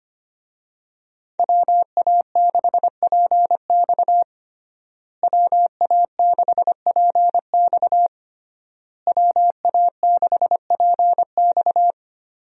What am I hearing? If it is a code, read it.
Morse code, heard as a keyed tone, about 25 wpm, 700 Hz. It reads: WA6PX WA6PX WA6PX
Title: USA Call Signs - 5 Characters - Repeated 3x - 25wpm